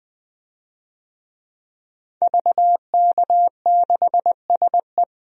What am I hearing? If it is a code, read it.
Morse code, heard as a keyed tone, about 20 wpm, 700 Hz. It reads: VK6SE
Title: DXCC-100 Call Signs - 5 Characters - 20wpm